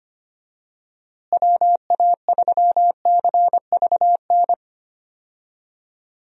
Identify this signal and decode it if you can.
Morse code, heard as a keyed tone, about 25 wpm, 700 Hz. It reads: WA3CVN
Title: USA Call Signs - 6 Characters - Repeated 3x - 25wpm